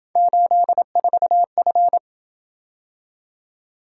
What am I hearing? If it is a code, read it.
Morse code, heard as a keyed tone, about 27 wpm, 700 Hz. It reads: 84F